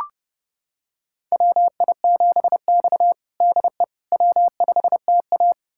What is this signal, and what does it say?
Morse code, heard as a keyed tone, about 30 wpm, 700 Hz. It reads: WI7X DE W5TA
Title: USA Callsign DE Callsign - 4 Characters - 30wpm